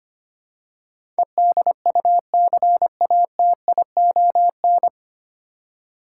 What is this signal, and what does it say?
Morse code, heard as a keyed tone, about 25 wpm, 700 Hz. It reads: EDUCATION
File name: Nine Letter Words - 25wpm